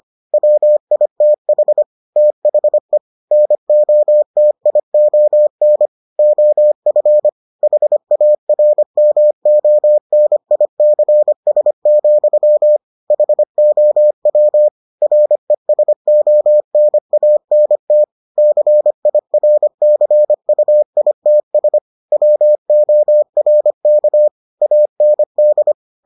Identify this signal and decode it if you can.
Morse code, heard as a keyed tone, about 25 wpm, 600 Hz. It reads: WITH THE NOTION OF HARMONICS, HOW RESONANT CIRCUITS WORK AND